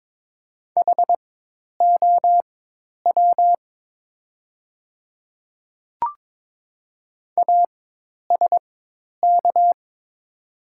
Morse code, heard as a keyed tone, about 22 wpm, spaced as for 10 wpm, 700 Hz. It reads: HOW ASK
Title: 4x Character Spacing - Top 300 Words - 22wpm